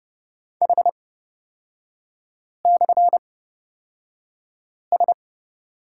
Morse code, heard as a keyed tone, about 30 wpm, spaced as for 5 wpm, 700 Hz. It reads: H/S
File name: Lesson 36e - BT - Three Characters 30wpm